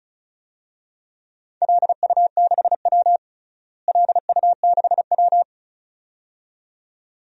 Morse code, heard as a keyed tone, about 34 wpm, 700 Hz. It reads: LU6W LU6W
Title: DXCC-100 Call Signs - 4 Characters - Repeated 2x - 35wpm